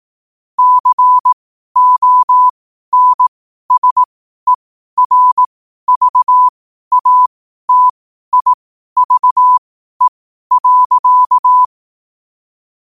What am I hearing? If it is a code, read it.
Morse code, heard as a keyed tone, about 18 wpm, spaced as for 12 wpm, 1000 Hz. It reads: CONSERVATIVE.
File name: qotdfiles / 12wpm